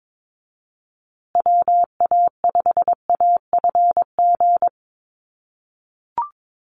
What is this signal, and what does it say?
Morse code, heard as a keyed tone, about 22 wpm, 700 Hz. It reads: WA5AFG
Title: USA Call Signs Repeated 2x 22wpm